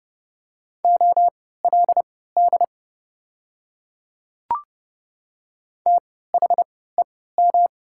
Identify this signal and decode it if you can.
Morse code, heard as a keyed tone, about 30 wpm, spaced as for 17 wpm, 700 Hz. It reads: OLD THEM
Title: 3x Character Spacing - Top 200 Words - 30wpm